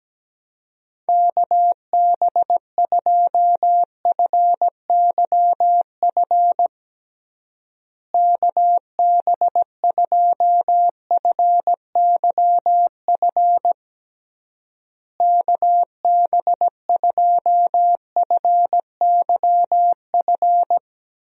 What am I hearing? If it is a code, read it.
Morse code, heard as a keyed tone, about 17 wpm, 700 Hz. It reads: KB2FYF KB2FYF KB2FYF
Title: USA Call Signs - 6 Characters - Repeated 3x - 17wpm